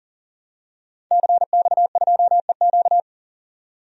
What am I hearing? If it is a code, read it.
Morse code, heard as a keyed tone, about 40 wpm, 700 Hz. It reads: CX2EQ
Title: DXCC-100 Call Signs - 5 Characters - 40wpm